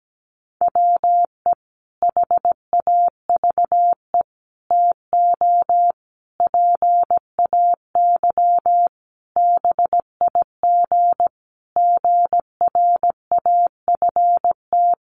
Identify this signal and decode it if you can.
Morse code, heard as a keyed tone, about 17 wpm, 700 Hz. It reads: WE HAVE TO PAY BIG GRAFT
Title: A Sound Of Thunder 17wpm